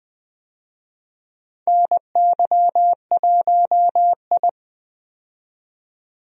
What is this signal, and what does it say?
Morse code, heard as a keyed tone, about 20 wpm, 700 Hz. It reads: NY1I
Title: USA Call Signs - 4 Characters - 20wpm